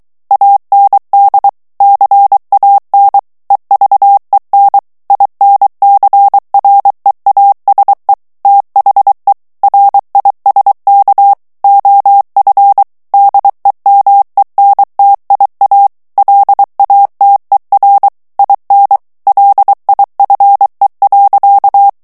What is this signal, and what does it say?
Morse code, heard as a keyed tone, about 23 wpm, 800 Hz. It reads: AND CAN EVEN INCREASE THE RISK OF DEMENTIA LATER IN LIFE.
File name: morse mp3 / ANGLAIS texte clair 1 vitesse 22 mots minute